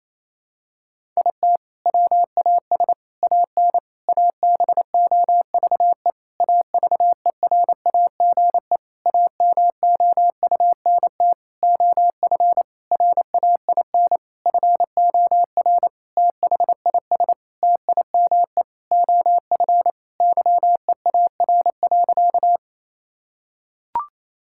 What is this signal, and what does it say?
Morse code, heard as a keyed tone, about 28 wpm, 700 Hz. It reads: IT WAS AN ABOVE AVERAGE AMOUNT OF RAIN FOR THIS TIME OF YEAR.
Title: Sentences from Top 1000 Words - Review Part 6 - 28wpm